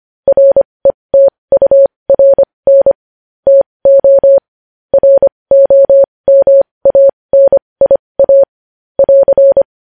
Morse code, heard as a keyed tone, about 25 wpm, spaced as for 20 wpm, 550 Hz. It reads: RETURN TO ROMANIA <AR>
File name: podcast / News headlines-1x-25x20